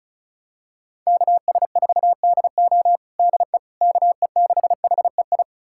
Morse code, heard as a keyed tone, about 34 wpm, 700 Hz. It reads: KS4DO DE KE6HEI
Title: USA Callsign DE Callsign 35wpm